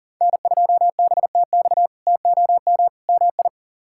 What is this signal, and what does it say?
Morse code, heard as a keyed tone, about 38 wpm, 700 Hz. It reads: N2BTX TOM MI
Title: K1USN SST QSO Practice - 40wpm